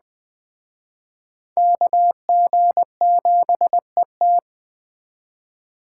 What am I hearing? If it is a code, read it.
Morse code, heard as a keyed tone, about 20 wpm, 700 Hz. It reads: KG7ET
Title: Speed Racing - USA 2x2 Call Signs 17wpm